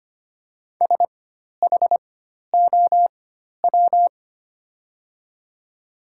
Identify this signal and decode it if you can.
Morse code, heard as a keyed tone, about 25 wpm, spaced as for 12 wpm, 700 Hz. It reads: SHOW
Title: 4x Character Spacing - Top 200 Words - 25wpm